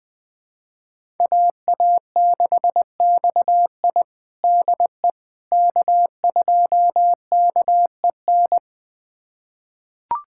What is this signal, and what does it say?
Morse code, heard as a keyed tone, about 20 wpm, 700 Hz. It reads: AA6XI DE K2KEN
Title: USA Callsign DE Callsign - 5 Characters - 20wpm